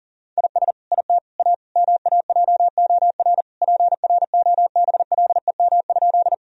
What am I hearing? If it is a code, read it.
Morse code, heard as a keyed tone, about 40 wpm, 700 Hz. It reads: IS IT A MAJOR PROBLEM?
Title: Sentences from Top 1000 Words - Review Part 4 - 40wpm